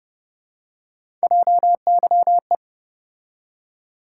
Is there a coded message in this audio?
Yes: Morse code from 1.2 to 2.5 s, heard as a keyed tone, about 32 wpm, 700 Hz. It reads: JYE